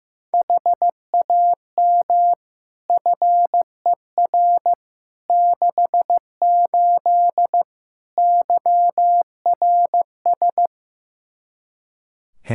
Morse code, heard as a keyed tone, about 15 wpm, 700 Hz. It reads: HAM FER 68 YRS
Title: K1USN SST QSO Practice 15wpm